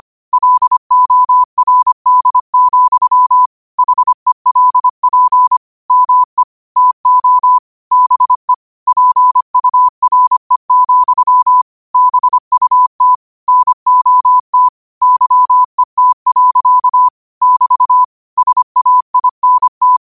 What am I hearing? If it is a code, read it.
Morse code, heard as a keyed tone, about 25 wpm, 1000 Hz. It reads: LORD, HELP ME TO BE PURE, BUT NOT YET. <BT> SAINT